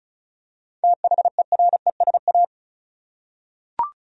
Morse code, heard as a keyed tone, about 34 wpm, 700 Hz. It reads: THERESA
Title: USA Common Names - 35wpm